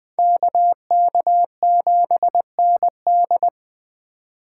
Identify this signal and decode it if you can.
Morse code, heard as a keyed tone, about 20 wpm, 700 Hz. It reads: KK7ND